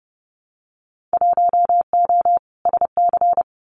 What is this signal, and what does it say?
Morse code, heard as a keyed tone, about 30 wpm, 700 Hz. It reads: 1O SC
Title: Winter Field Day Exchanges 30wpm